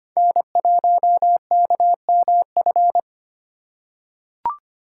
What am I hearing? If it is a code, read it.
Morse code, heard as a keyed tone, about 25 wpm, 700 Hz. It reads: N1KMF
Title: USA Call Signs - 3x Character Spacing - 25wpm